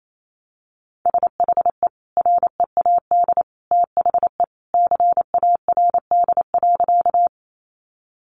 Morse code, heard as a keyed tone, about 28 wpm, 700 Hz. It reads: SHE READ THE CARD.
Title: Sentences from Top 600 Words 28wpm